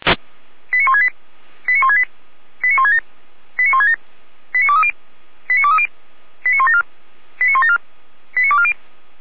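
Five-tone selective call (ZVEI), heard as quick runs of five tones, tones of about 80 ms. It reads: 89168 89168 89167 89167 89230 89230 89174 89174 89240
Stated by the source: ZVEI five-tone selective call received via SDR.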